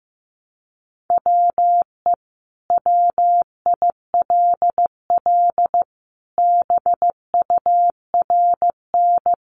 Morse code, heard as a keyed tone, about 15 wpm, 700 Hz. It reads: WE WILL BURN
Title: Sentences from Top 1300 Words 15wpm